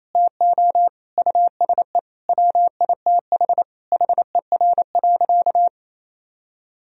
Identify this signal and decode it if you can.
Morse code, heard as a keyed tone, about 28 wpm, 700 Hz. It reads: TO USE WITH HER.
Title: Sentences from Top 1000 Words - Review Part 4 - 28wpm